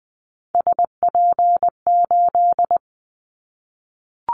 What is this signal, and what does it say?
Morse code, heard as a keyed tone, about 20 wpm, 700 Hz. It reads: SP8
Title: Callsign Prefixes for Top 25 DXCC - 20wpm